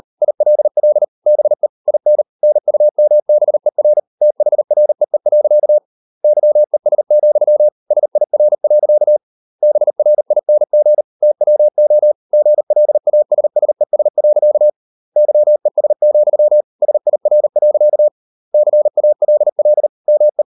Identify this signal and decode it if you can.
Morse code, heard as a keyed tone, about 38 wpm, 600 Hz. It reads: ILL BE IN NUMBER THREE. YES, SIR. BRING TWO GLASSES. YES, SIR. CALL ME